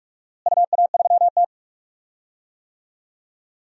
Morse code, heard as a keyed tone, about 45 wpm, 700 Hz. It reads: UA3T
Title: DXCC-100 Call Signs - 4 Characters - Repeated 3x - 45wpm